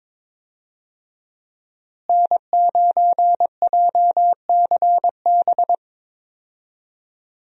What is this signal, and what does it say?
Morse code, heard as a keyed tone, about 22 wpm, 700 Hz. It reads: N9JCB
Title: USA Call Signs - 5 Characters - Repeated 3x - 22wpm